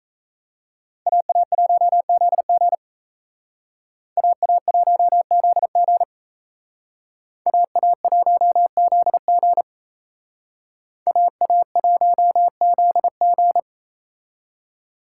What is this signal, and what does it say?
Morse code, heard as a keyed tone, about 35 wpm, 700 Hz. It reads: AA1ZG AA1ZG AA1ZG AA1ZG
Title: Speed Racing - USA 2x2 Call Signs 28wpm